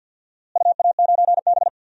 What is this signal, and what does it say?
Morse code, heard as a keyed tone, about 53 wpm, 700 Hz. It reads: UA9B
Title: DXCC-100 Call Signs - 4 Characters - Repeated 2x - 50wpm